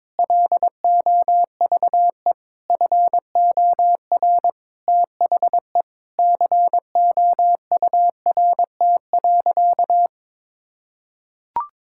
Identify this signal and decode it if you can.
Morse code, heard as a keyed tone, about 22 wpm, 700 Hz. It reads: LOVE FOR THE COURT.
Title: Sentences from Top 1600 Words 22wpm